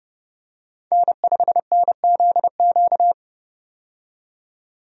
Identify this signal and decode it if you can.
Morse code, heard as a keyed tone, about 32 wpm, 700 Hz. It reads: N5NZQ
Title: USA Call Signs - 5 Characters - 30wpm